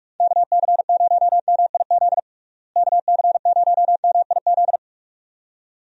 Morse code, heard as a keyed tone, about 46 wpm, 700 Hz. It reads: KC0MIZ KC0MIZ
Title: USA Call Signs - 6 Characters - Repeated 2x - 45wpm